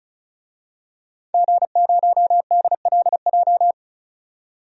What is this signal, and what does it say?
Morse code, heard as a keyed tone, about 34 wpm, 700 Hz. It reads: G0DLJ
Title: DXCC-100 Call Signs - 5 Characters - 35wpm